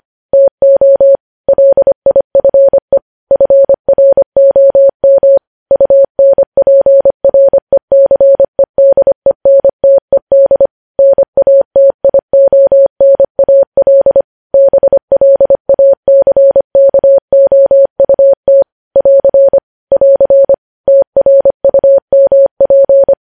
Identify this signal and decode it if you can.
Morse code, heard as a keyed tone, about 25 wpm, 550 Hz. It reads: TO LIFE FROM UNPRECEDENTED NATIONAL BLACKOUT <AR> <AR> TRUMP
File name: podcast / News headlines-1x-25x25